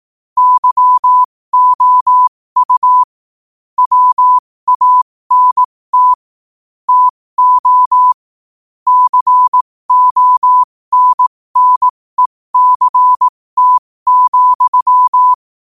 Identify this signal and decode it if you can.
Morse code, heard as a keyed tone, about 18 wpm, spaced as for 15 wpm, 1000 Hz. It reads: YOU WANT TO CONNECT,
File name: qotdfiles / 15wpm